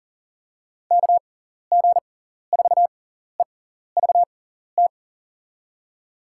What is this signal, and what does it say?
Morse code, heard as a keyed tone, about 40 wpm, spaced as for 14 wpm, 700 Hz. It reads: KG4EVT
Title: USA Call Signs - 6 Characters - 6x Character Spacing - 40wpm